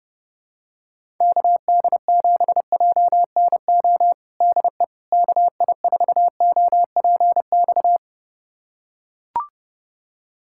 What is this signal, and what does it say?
Morse code, heard as a keyed tone, about 30 wpm, 700 Hz. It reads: KD7JNO DE KI4OPX
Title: USA Callsign DE Callsign 30wpm